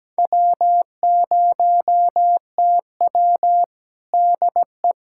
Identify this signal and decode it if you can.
Morse code, heard as a keyed tone, about 17 wpm, 700 Hz. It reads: W0TW DE
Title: USA Callsign DE Callsign - 4 Characters - 17wpm